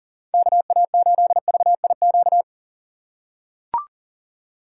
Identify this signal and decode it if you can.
Morse code, heard as a keyed tone, about 39 wpm, 700 Hz. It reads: KA8VIQ
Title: USA Call Signs - 6 Characters - 4x Character Spacing - 40wpm